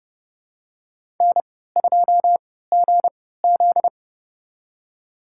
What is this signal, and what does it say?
Morse code, heard as a keyed tone, about 31 wpm, spaced as for 17 wpm, 690 Hz. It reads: N2GZ
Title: USA Call Signs - 3x Character Spacing - 30wpm